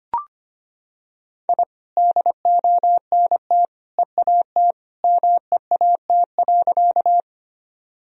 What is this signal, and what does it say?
Morse code, heard as a keyed tone, about 25 wpm, 700 Hz. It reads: I DONT EAT MEAT.
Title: Sentences from Top 1000 Words - Review Part 3 - 25wpm